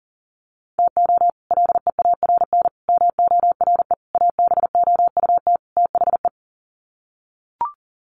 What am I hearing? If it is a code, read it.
Morse code, heard as a keyed tone, about 40 wpm, 700 Hz. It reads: TO LEARN MORE ABOUT THE
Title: Sets of 5 Words Encore2 40wpm